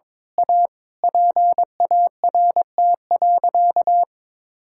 Morse code, heard as a keyed tone, about 21 wpm, 700 Hz. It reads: A PART.
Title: Sentences from Top 1000 Words - Review Part 5 - 22wpm